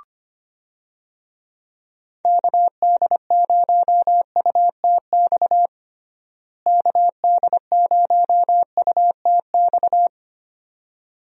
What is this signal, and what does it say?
Morse code, heard as a keyed tone, about 25 wpm, 700 Hz. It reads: KD0UTX KD0UTX
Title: USA Call Signs - 6 Characters - Repeated 2x - 25wpm